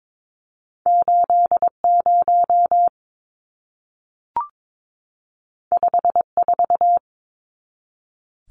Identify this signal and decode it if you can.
Morse code, heard as a keyed tone, about 22 wpm, 700 Hz. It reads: 80 54
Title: Numbers - 2-Digit 22wpm